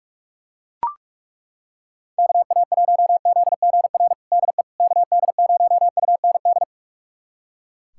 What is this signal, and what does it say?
Morse code, heard as a keyed tone, about 41 wpm, 700 Hz. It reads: KA1ZGR DE KD0UND
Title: USA Callsign DE Callsign - 6 Characters - 45wpm